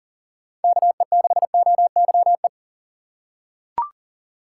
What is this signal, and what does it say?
Morse code, heard as a keyed tone, about 40 wpm, 700 Hz. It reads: KE6OYE
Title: USA Call Signs - 4x Character Spacing - 40wpm